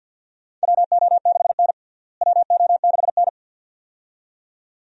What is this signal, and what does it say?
Morse code, heard as a keyed tone, about 51 wpm, 700 Hz. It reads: WO6N WO6N
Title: USA Call Signs - 4 Characters - Repeated 2x - 50wpm